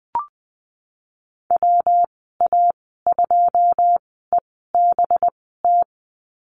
Morse code, heard as a keyed tone, about 20 wpm, spaced as for 15 wpm, 700 Hz. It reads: WA2EBT
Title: 186 Lesson 30g - Letter b - Call Signs 20wpm